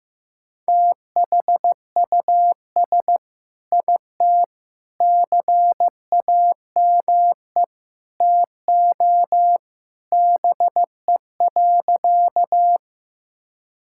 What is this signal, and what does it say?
Morse code, heard as a keyed tone, about 15 wpm, 690 Hz. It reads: THUS IT CAME TO BE.
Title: Sentences from Top 1000 Words - Review Part 5 - 15wpm